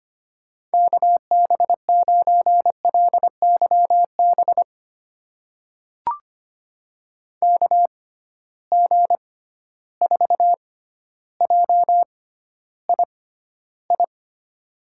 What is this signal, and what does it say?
Morse code, heard as a keyed tone, about 25 wpm, spaced as for 13 wpm, 700 Hz. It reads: KB9LYB KG4JII
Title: USA Call Signs - 6 Characters - 6x Character Spacing - 25wpm